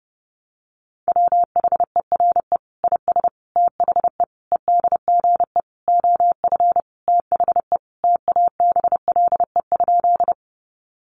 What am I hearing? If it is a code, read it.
Morse code, heard as a keyed tone, about 30 wpm, 700 Hz. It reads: WHERE IS THE EDGE OF THE TABLE?
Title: Sentences from Top 1300 Words 30wpm